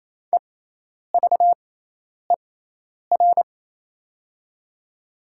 Morse code, heard as a keyed tone, about 28 wpm, spaced as for 10 wpm, 700 Hz. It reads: EVER
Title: 6x Character Spacing - Top 500 Words - 28wpm